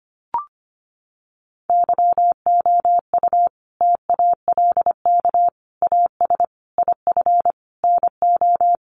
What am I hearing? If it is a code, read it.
Morse code, heard as a keyed tone, about 25 wpm, 700 Hz. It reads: YOU TALK AS IF NO
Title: Sentences from Top 1100 Words 25wpm